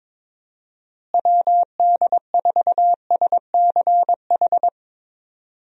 Morse code, heard as a keyed tone, about 22 wpm, 700 Hz. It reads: WD4SCH
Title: USA Call Signs - 6 Characters - Repeated 2x - 22wpm